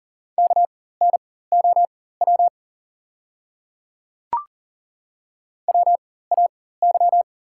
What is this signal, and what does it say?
Morse code, heard as a keyed tone, about 38 wpm, spaced as for 18 wpm, 700 Hz. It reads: KNOW WAY
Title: 4x Character Spacing - Top 100 Words - 40wpm